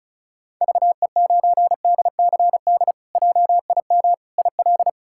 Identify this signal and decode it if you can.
Morse code, heard as a keyed tone, about 36 wpm, 700 Hz. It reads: VE9DCD JIM IL
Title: K1USN SST QSO Practice - 35wpm